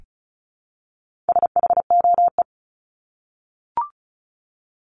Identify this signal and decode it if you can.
Morse code, heard as a keyed tone, about 34 wpm, 700 Hz. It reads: SHOE
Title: Top 1751-2000 Words 35wpm